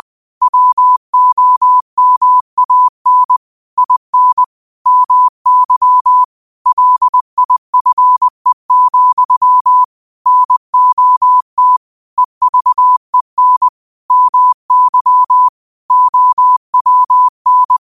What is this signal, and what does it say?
Morse code, heard as a keyed tone, about 20 wpm, 1000 Hz. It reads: WOMAN IN MY LIFE, NOT EVEN MY OWN
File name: 20wpm